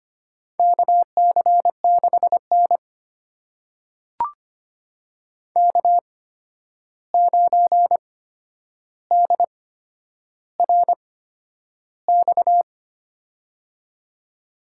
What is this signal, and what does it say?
Morse code, heard as a keyed tone, about 25 wpm, spaced as for 10 wpm, 700 Hz. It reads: KC6N K9DRX
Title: USA Call Signs - 8x Character Spacing - 25wpm